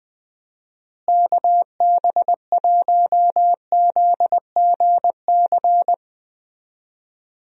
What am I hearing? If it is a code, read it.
Morse code, heard as a keyed tone, about 20 wpm, 700 Hz. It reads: KB1ZGC
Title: USA Call Signs - 6 Characters - Repeated 2x - 20wpm